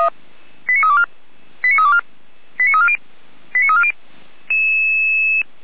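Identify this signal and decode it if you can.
Five-tone selective call (ZVEI), heard as quick runs of five tones, tones of about 70 ms. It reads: 89324 89324 89340 89340
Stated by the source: Received by SDR, logged on a server of Feuerwehr Orlamünde (fire brigade).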